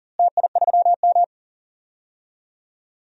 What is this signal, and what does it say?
Morse code, heard as a keyed tone, about 39 wpm, 700 Hz. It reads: TI3M
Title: DXCC-100 Call Signs - 4 Characters - Repeated 2x - 40wpm